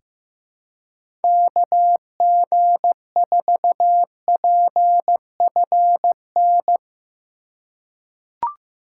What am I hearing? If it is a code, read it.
Morse code, heard as a keyed tone, about 16 wpm, 700 Hz. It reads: KG4PFN
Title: USA Call Signs - 6 Characters - 4x Character Spacing - 15wpm